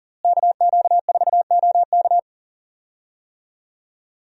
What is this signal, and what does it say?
Morse code, heard as a keyed tone, about 37 wpm, 700 Hz. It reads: KQ4OK
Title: USA Call Signs - 5 Characters - Repeated 3x - 40wpm